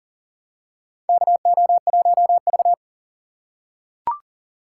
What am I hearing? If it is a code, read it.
Morse code, heard as a keyed tone, about 40 wpm, 700 Hz. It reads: KO1V
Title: USA Call Signs - 4 Characters - 3x Character Spacing - 40wpm